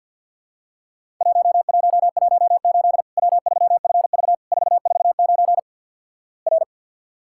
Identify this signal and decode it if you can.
Morse code, heard as a keyed tone, about 50 wpm, 700 Hz. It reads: 1118 W3FV 449 R
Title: Worked All Europe DX Contest - QTC Exchange with Repeats - 50wpm